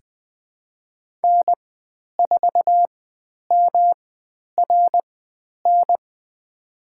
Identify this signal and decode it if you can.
Morse code, heard as a keyed tone, about 20 wpm, spaced as for 10 wpm, 700 Hz. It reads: N4MRN
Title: 109 Lesson 18g - Letter m - Call Signs 20wpm